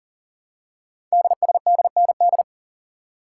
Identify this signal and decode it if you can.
Morse code, heard as a keyed tone, about 42 wpm, 700 Hz. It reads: DSDND
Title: Code Groups - 5 Letters - 40wpm